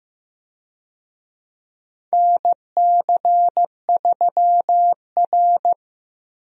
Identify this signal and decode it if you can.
Morse code, heard as a keyed tone, about 15 wpm, 700 Hz. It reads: NC3R
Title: USA Call Signs - 4 Characters - Repeated 2x - 15wpm